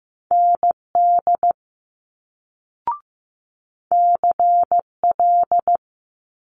Morse code, heard as a keyed tone, about 15 wpm, 700 Hz. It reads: ND CL
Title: Multi-letter Phonemes Quick Fire 15wpm